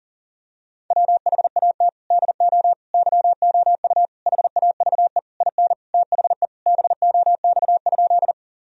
Morse code, heard as a keyed tone, about 40 wpm, 700 Hz. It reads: WHAT DO YOU HAVE IN THE BOX?
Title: Sentences from Top 1000 Words - Review Part 2 - 40wpm